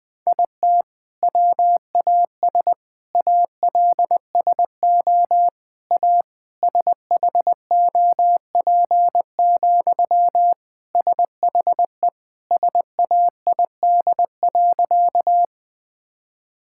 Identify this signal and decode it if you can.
Morse code, heard as a keyed tone, about 20 wpm, 700 Hz. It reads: IT WAS ALSO A SHOP, SHE SAID.
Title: Sentences from Top 1600 Words 20wpm